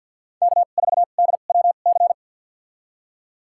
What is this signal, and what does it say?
Morse code, heard as a keyed tone, about 53 wpm, spaced as for 37 wpm, 700 Hz. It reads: K4DWC
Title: USA Call Signs - 5 Characters - 2x Character Spacing - 50wpm